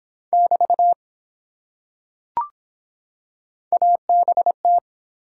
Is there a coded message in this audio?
Yes: Morse code from 0.3 to 4.8 s, heard as a keyed tone, about 23 wpm, 700 Hz. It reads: <BT> ABT